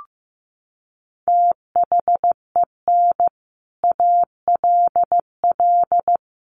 Morse code, heard as a keyed tone, about 15 wpm, 700 Hz. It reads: THEN ALL
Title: Sets of 5 Words Encore 15wpm